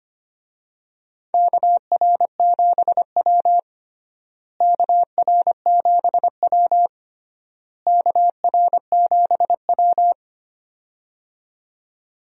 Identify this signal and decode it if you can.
Morse code, heard as a keyed tone, about 25 wpm, 700 Hz. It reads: KR7W KR7W KR7W
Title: USA Call Signs - 4 Characters - Repeated 3x - 25wpm